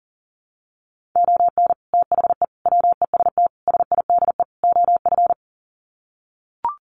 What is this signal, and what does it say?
Morse code, heard as a keyed tone, about 40 wpm, 700 Hz. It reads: ON THE WEST SIDE OF
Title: Sets of 5 Words Encore2 40wpm